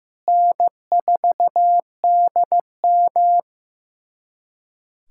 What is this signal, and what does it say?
Morse code, heard as a keyed tone, about 15 wpm, 700 Hz. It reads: N4DM
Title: USA Callsign DE Callsign - 4 Characters - 15wpm